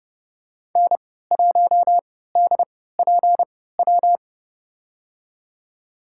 Morse code, heard as a keyed tone, about 30 wpm, spaced as for 17 wpm, 700 Hz. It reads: N1DPW
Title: USA Call Signs - 5 Characters - 3x Character Spacing - 30wpm